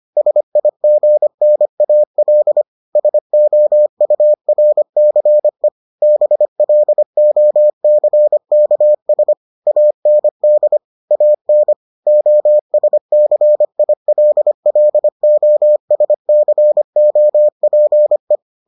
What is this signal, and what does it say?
Morse code, heard as a keyed tone, about 25 wpm, 600 Hz. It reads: SIGNAL SOURCE BLOCKS AND AN OSCILLOSCOPE